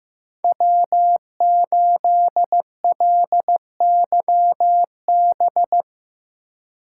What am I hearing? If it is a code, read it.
Morse code, heard as a keyed tone, about 15 wpm, 700 Hz. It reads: W8LYB
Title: USA Call Signs - 5 Characters - 4x Character Spacing - 15wpm